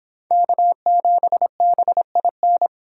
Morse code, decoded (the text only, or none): K7BIN